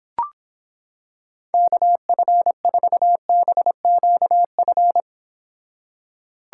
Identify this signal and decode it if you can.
Morse code, heard as a keyed tone, about 26 wpm, 700 Hz. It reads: KF4BQF